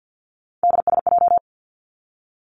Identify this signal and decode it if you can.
Morse code, heard as a keyed tone, about 50 wpm, 700 Hz. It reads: DSJ
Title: Three Challenging Letter-Number Combinations 50wpm